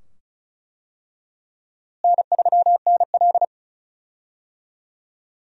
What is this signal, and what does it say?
Morse code, heard as a keyed tone, about 34 wpm, 700 Hz. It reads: N3NL
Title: USA Call Signs - 4 Characters - Repeated 2x - 35wpm